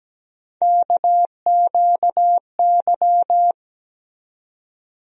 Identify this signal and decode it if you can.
Morse code, heard as a keyed tone, about 17 wpm, 700 Hz. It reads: KQY
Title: USA Call Sign Postfixes - 3 Letters 17wpm